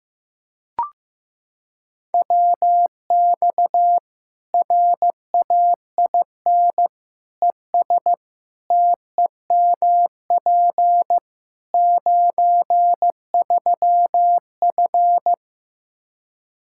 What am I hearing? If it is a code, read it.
Morse code, heard as a keyed tone, about 15 wpm, 700 Hz. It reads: WX RAIN ES TEMP 93F
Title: Practice QSO - Exchange 2 - Info - 15wpm